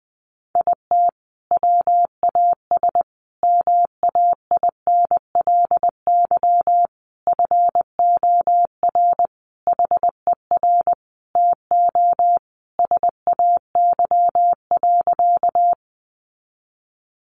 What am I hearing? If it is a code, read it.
Morse code, heard as a keyed tone, about 20 wpm, 700 Hz. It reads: IT WAS MAINLY FOR HER TO SAY.
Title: Sentences from Top 1400 Words 20wpm